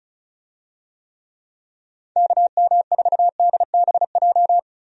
Morse code, heard as a keyed tone, about 34 wpm, 700 Hz. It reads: KM4DBJ
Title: USA Call Signs - 6 Characters - Repeated 2x - 35wpm